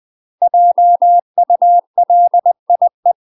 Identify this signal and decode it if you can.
Morse code, heard as a keyed tone, about 20 wpm, 700 Hz. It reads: JULIE
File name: UK MASTER QSOs TRAINING / 20WPM / UK MASTER QSOs 0052